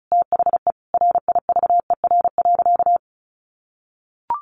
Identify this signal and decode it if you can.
Morse code, heard as a keyed tone, about 34 wpm, 700 Hz. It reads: THE RIVER.
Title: Sentences from Top 1500 Words 35wpm